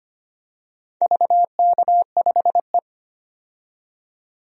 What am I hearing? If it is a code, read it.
Morse code, heard as a keyed tone, about 25 wpm, 700 Hz. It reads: VK5E
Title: DXCC-100 Call Signs - 4 Characters - 25wpm